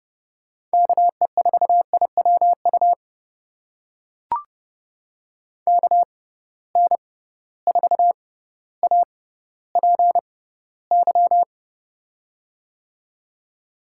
Morse code, heard as a keyed tone, about 32 wpm, spaced as for 16 wpm, 700 Hz. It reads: KE4IWU KN4APY
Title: USA Call Signs - 6 Characters - 6x Character Spacing - 30wpm